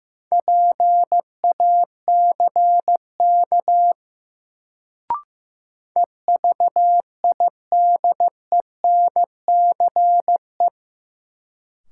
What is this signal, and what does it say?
Morse code, heard as a keyed tone, about 15 wpm, 700 Hz. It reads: PACK EVIDENCE
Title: Sentences from Top 1300 Words - New Word Drill Alt - 15wpm